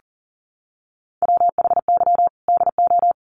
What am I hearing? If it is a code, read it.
Morse code, heard as a keyed tone, about 40 wpm, 700 Hz. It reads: WHY DO
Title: Sets of 5 Words Encore 40wpm